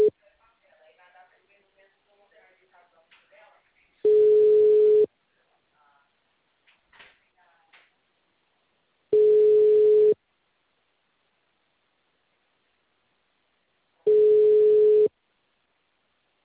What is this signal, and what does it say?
Ringback tone (Europe, 425 Hz), with speech.
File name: out- -2012- - - .13508